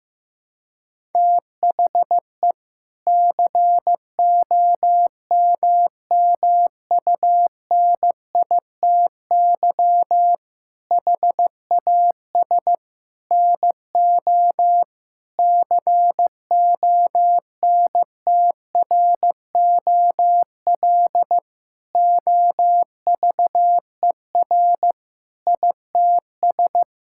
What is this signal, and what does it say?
Morse code, heard as a keyed tone, about 15 wpm, 700 Hz. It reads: THE COMMUNITY HAS NO CONTROL OVER ITS
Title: Sentences from Top 1000 Words - Review Part 5 - 15wpm